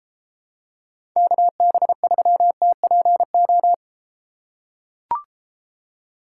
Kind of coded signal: Morse code

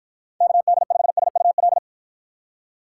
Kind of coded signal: Morse code

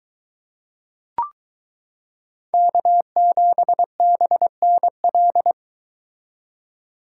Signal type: Morse code

